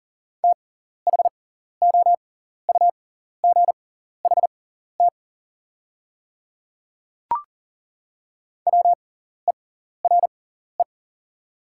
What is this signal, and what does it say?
Morse code, heard as a keyed tone, about 37 wpm, spaced as for 14 wpm, 690 Hz. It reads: THOUGHT WERE
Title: 6x Character Spacing - Top 500 Words - 40wpm